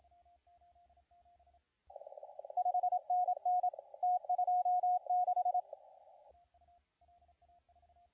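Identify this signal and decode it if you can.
Morse code, heard as a keyed tone, about 27 wpm, 720 Hz, with radio noise. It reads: 5NN T26
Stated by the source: CW contest audio recorded at LA4C during the SAC CW contest.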